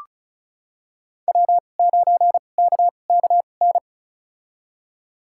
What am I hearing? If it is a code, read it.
Morse code, heard as a keyed tone, about 36 wpm, spaced as for 25 wpm, 700 Hz. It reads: W9KKN
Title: USA Call Signs - 5 Characters - 2x Character Spacing - 35wpm